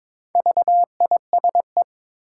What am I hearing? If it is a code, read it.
Morse code, heard as a keyed tone, about 22 wpm, 690 Hz. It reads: VISE
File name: Difficult Words with Hints - 22wpm